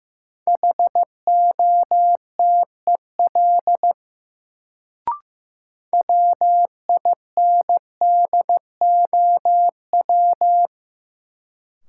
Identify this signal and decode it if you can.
Morse code, heard as a keyed tone, about 15 wpm, 700 Hz. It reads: HOTEL WINDOW
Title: Sentences from Top 1600 Words - New Word Drill - 15wpm